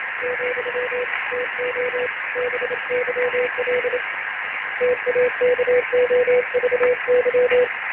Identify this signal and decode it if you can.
Morse code, heard as a keyed tone, about 27 wpm, 500 Hz. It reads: , TO BYL TAKOVY
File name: ARCHIV / BPSK CW / OK1IF-OK1ANG- -1230CET-3766.5-3767